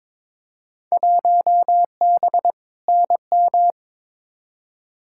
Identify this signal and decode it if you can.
Morse code, heard as a keyed tone, about 22 wpm, 700 Hz. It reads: 1B NM
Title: Field Day Exchanges Encore - 22wpm